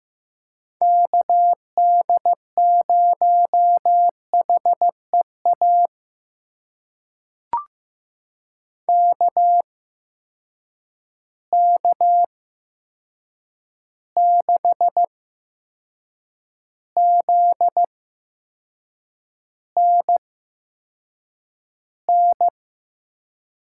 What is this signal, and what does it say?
Morse code, heard as a keyed tone, about 15 wpm, spaced as for 6 wpm, 700 Hz. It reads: KD0HEA KK6ZNN
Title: USA Call Signs - 6 Characters - 8x Character Spacing - 15wpm